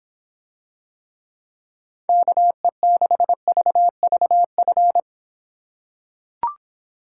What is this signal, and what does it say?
Morse code, heard as a keyed tone, about 26 wpm, 700 Hz. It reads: KE6VVF